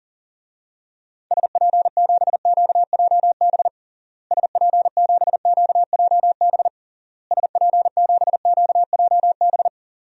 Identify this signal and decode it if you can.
Morse code, heard as a keyed tone, about 40 wpm, 700 Hz. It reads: SP7QJB SP7QJB SP7QJB
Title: DXCC-100 Call Signs - 6 Characters - Repeated 3x - 40wpm